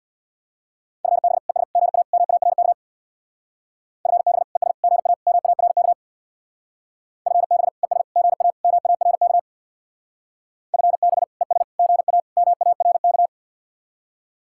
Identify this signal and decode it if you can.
Morse code, heard as a keyed tone, about 66 wpm, 700 Hz. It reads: FB ES GA MARK FB ES GA MARK FB ES GA MARK FB ES GA MARK